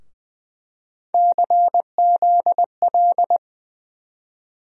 Morse code, heard as a keyed tone, about 20 wpm, 700 Hz. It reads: CZL
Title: USA Call Sign Postfixes - 3 Letters 20wpm